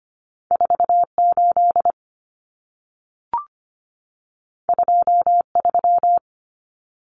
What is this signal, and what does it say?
Morse code, heard as a keyed tone, about 25 wpm, 700 Hz. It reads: 48 23
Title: Common Cut Numbers - 2-Digit 25wpm